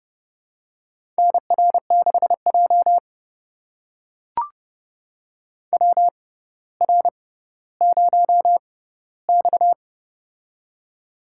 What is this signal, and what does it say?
Morse code, heard as a keyed tone, about 30 wpm, spaced as for 15 wpm, 700 Hz. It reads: NR6J WR0X
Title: USA Call Signs - 4 Characters - 6x Character Spacing - 30wpm